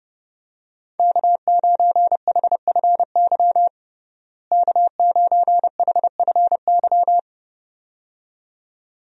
Morse code, heard as a keyed tone, about 30 wpm, 700 Hz. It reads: K9HFY K9HFY
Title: USA Call Signs - 5 Characters - Repeated 2x - 30wpm